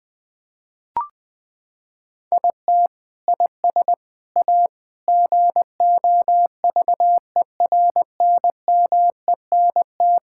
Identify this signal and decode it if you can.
Morse code, heard as a keyed tone, about 20 wpm, 700 Hz. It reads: IT IS A GOVERNMENT
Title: Sentences from Top 1000 Words - Review Part 5 - 20wpm